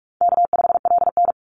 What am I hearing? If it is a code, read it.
Morse code, heard as a keyed tone, about 44 wpm, 700 Hz. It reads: K5LN
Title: Call Signs - 45wpm